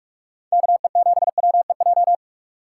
Morse code, heard as a keyed tone, about 46 wpm, 700 Hz. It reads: KE7WEJ
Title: USA Call Signs - 6 Characters - 4x Character Spacing - 45wpm